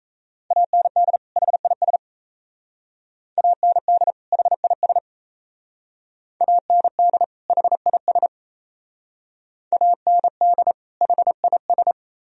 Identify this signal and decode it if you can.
Morse code, heard as a keyed tone, about 36 wpm, 700 Hz. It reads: AND HIS AND HIS AND HIS AND HIS